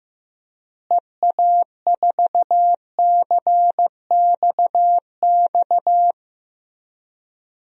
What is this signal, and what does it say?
Morse code, heard as a keyed tone, about 15 wpm, 700 Hz. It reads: EA4CXX
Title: DXCC-100 Call Signs - 6 Characters - Repeated 3x - 15wpm